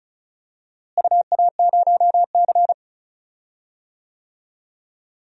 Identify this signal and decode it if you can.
Morse code, heard as a keyed tone, about 36 wpm, 700 Hz. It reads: UA0C
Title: DXCC-100 Call Signs - 4 Characters - Repeated 3x - 35wpm